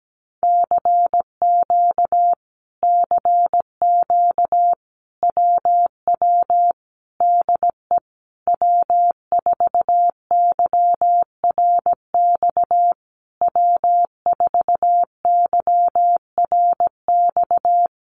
Morse code, heard as a keyed tone, about 17 wpm, 700 Hz. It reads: CQ CQ WW DE W4YRX W4YRX
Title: Common QSO Phrases Encore 17wpm